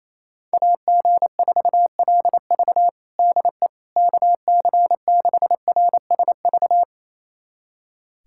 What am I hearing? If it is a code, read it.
Morse code, heard as a keyed tone, about 28 wpm, 700 Hz. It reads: AG4LV DE KC6RSV
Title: USA Callsign DE Callsign 28wpm